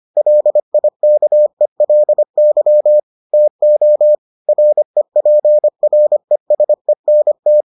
Morse code, heard as a keyed tone, about 25 wpm, 600 Hz. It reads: LIKELY TO REPRESENT